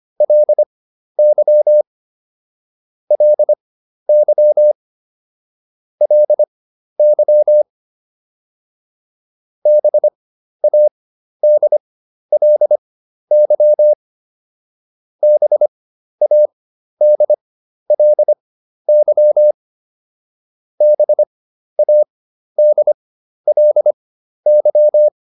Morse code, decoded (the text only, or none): LY LY LY BADLY BADLY BADLY